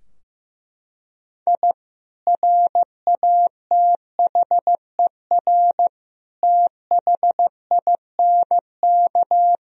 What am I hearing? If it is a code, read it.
Morse code, heard as a keyed tone, about 15 wpm, 700 Hz. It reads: I RATHER THINK
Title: Sentences from Top 1000 Words - Review Part 4 - 15wpm